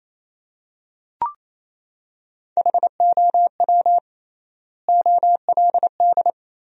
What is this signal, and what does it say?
Morse code, heard as a keyed tone, about 28 wpm, 700 Hz. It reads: HOW OLD
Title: 3x Word-Spacing - Sentences from Top 200 Words - 28wpm